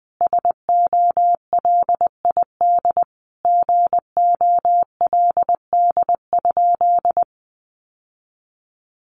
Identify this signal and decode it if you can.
Morse code, heard as a keyed tone, about 20 wpm, 700 Hz. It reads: SOLID GOLD?
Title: Sentences from Top 1500 Words 20wpm